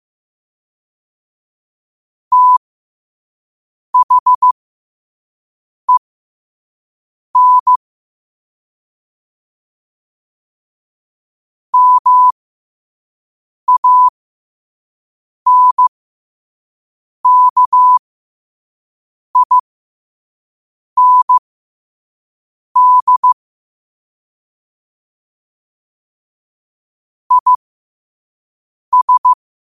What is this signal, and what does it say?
Morse code, heard as a keyed tone, about 15 wpm, spaced as for 5 wpm, 1000 Hz. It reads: THEN MANKIND IS
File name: qotdfiles / 5wpm